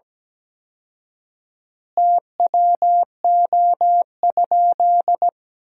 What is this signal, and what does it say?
Morse code, heard as a keyed tone, about 17 wpm, 700 Hz. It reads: TWO?
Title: 4x Word-Spacing - Sentences from Top 100 Words - 17wpm